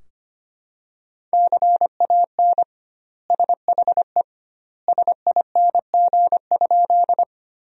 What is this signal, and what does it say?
Morse code, heard as a keyed tone, about 25 wpm, 700 Hz. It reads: CAN SHE SING?
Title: 2x Word-Spacing - Sentences from Top 400 Words - 25wpm